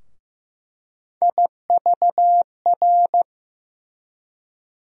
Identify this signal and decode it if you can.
Morse code, heard as a keyed tone, about 15 wpm, 700 Hz. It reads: IVR
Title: USA Call Sign Postfixes - 3 Letters 15wpm